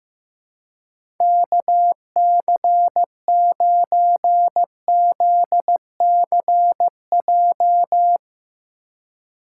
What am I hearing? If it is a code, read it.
Morse code, heard as a keyed tone, about 15 wpm, 700 Hz. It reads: KC9ZCJ
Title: USA Call Signs - 6 Characters - 2x Character Spacing - 15wpm